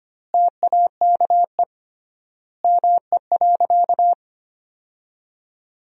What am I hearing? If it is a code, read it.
Morse code, heard as a keyed tone, about 25 wpm, 700 Hz. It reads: TAKE ME.
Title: 3x Word-Spacing - Sentences from Top 400 Words - 25wpm